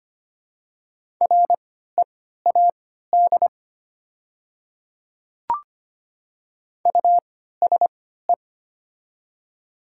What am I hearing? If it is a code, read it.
Morse code, heard as a keyed tone, about 25 wpm, spaced as for 14 wpm, 700 Hz. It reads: READ USE
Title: 3x Character Spacing - Top 200 Words - 25wpm